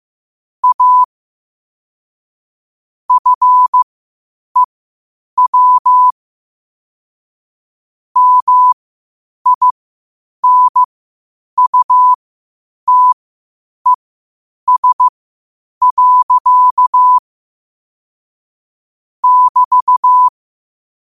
Morse code, heard as a keyed tone, about 15 wpm, spaced as for 8 wpm, 1000 Hz. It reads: A FEW MINUTES. <BT>
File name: qotdfiles / 8wpm